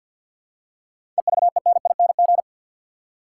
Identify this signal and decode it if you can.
Morse code, heard as a keyed tone, about 51 wpm, 700 Hz. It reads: EVENING
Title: Seven Letter Words - 50wpm